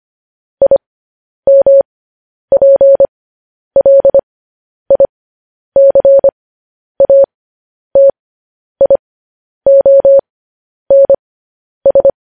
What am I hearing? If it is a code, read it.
Morse code, heard as a keyed tone, about 25 wpm, spaced as for 10 wpm, 550 Hz. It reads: IMPLICATIONS